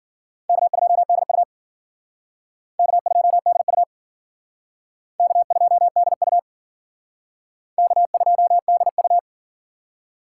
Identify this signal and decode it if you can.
Morse code, heard as a keyed tone, about 52 wpm, 700 Hz. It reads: K2DU K2DU K2DU K2DU